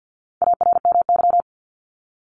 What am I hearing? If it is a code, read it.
Morse code, heard as a keyed tone, about 50 wpm, 700 Hz. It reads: ARMY
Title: Top 501-750 Words 50wpm